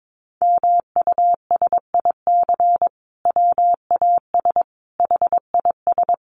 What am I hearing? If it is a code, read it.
Morse code, heard as a keyed tone, about 22 wpm, 700 Hz. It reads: MUSIC WAS HIS
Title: Sentences from Top 1300 Words 22wpm